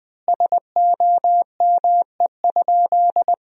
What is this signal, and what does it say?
Morse code, heard as a keyed tone, about 20 wpm, 700 Hz. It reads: SOME?
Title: 4x Word-Spacing - Sentences from Top 100 Words - 20wpm